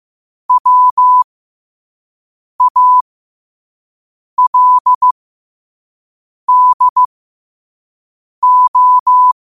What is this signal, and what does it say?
Morse code, heard as a keyed tone, about 15 wpm, spaced as for 5 wpm, 1000 Hz. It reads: WALDO
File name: qotdfiles / 5wpm